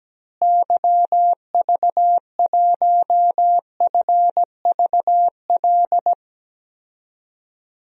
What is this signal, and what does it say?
Morse code, heard as a keyed tone, about 17 wpm, 700 Hz. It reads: YV1FVL
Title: DXCC-100 Call Signs - 6 Characters - Repeated 2x - 17wpm